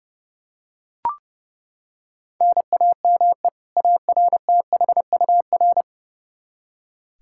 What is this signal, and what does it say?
Morse code, heard as a keyed tone, about 30 wpm, 700 Hz. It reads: NAME ARTHUR
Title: Practice QSO - Exchange 1 - Beginning - 30wpm